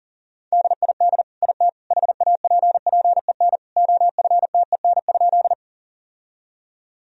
Morse code, heard as a keyed tone, about 40 wpm, 700 Hz. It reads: DID IT HAPPEN OFTEN?
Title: Sentences from Top 1000 Words - Review Part 1 - 40wpm